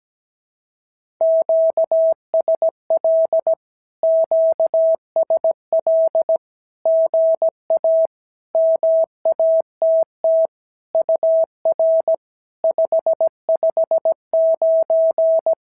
Morse code, heard as a keyed tone, about 17 wpm, 650 Hz. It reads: QSL QSL GA MATT UR 559
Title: SOTA QSO Practice 17wpm